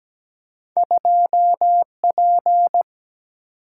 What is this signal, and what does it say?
Morse code, heard as a keyed tone, about 17 wpm, 700 Hz. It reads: 2P
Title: Single-Letter Single-Number 17wpm